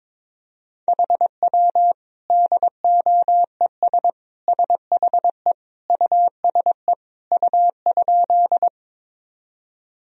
Morse code, heard as a keyed tone, about 22 wpm, 700 Hz. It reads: HW DOES SHE USE U?
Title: Sentences from Top 200 Words and Abbreviations 22wpm